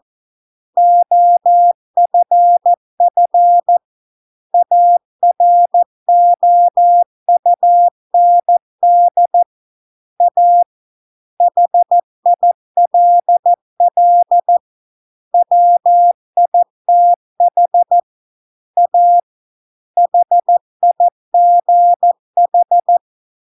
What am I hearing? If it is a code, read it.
Morse code, heard as a keyed tone, about 14 wpm, 700 Hz. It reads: OFF AROUND A HILL WITH A HIGH